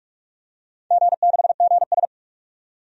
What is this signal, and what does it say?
Morse code, heard as a keyed tone, about 44 wpm, 690 Hz. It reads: G6GS
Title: DXCC-100 Call Signs - 4 Characters - 45wpm